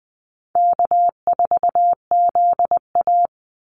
Morse code, heard as a keyed tone, about 20 wpm, 700 Hz. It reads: K4ZA
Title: Easy Call Signs Repeated 2x - 20wpm